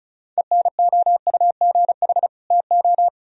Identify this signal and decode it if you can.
Morse code, heard as a keyed tone, about 43 wpm, 700 Hz. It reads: ENOUGH TO
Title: Speed-Racing - Sets of 2 Words 35wpm